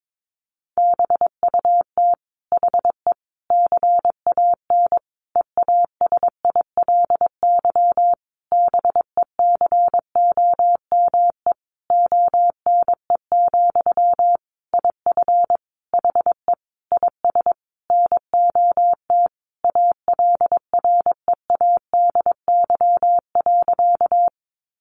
Morse code, heard as a keyed tone, about 22 wpm, 700 Hz. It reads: BUT HE CAN EASILY BECOME ONE, IF HE IS NOT ALREADY.
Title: Sentences from Top 900 Words 22wpm